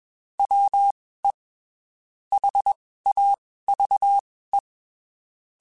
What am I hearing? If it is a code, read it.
Morse code, heard as a keyed tone, about 21 wpm, spaced as for 15 wpm, 800 Hz. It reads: WE HAVE
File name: (spaced) / QSOs Spaced-mp3s / WD0BGP 1 @20 sp